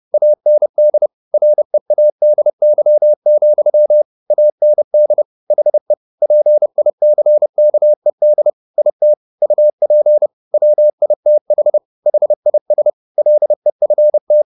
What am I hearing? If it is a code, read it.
Morse code, heard as a keyed tone, about 30 wpm, 600 Hz. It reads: AND READY, AND HE PICKED IT UP WITH HIS LEFT